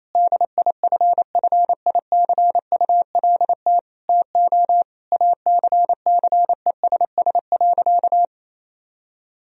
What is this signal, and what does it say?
Morse code, heard as a keyed tone, about 28 wpm, 700 Hz. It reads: DIFFICULT TO ACCESS.
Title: Sentences from Top 1000 Words - Review Part 4 - 28wpm